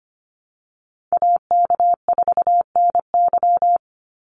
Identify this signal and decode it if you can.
Morse code, heard as a keyed tone, about 25 wpm, 700 Hz. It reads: AK4NY
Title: Call Signs Repeated 2x - 25wpm